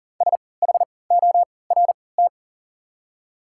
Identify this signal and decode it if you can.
Morse code, heard as a keyed tone, about 40 wpm, spaced as for 23 wpm, 700 Hz. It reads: SHORT
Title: 3x Character Spacing - Top 500 Words - 40wpm